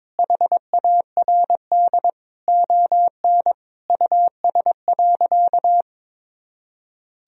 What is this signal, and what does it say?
Morse code, heard as a keyed tone, about 22 wpm, 700 Hz. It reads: HARD ON US.
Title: Sentences from Top 1000 Words - Review Part 1 - 22wpm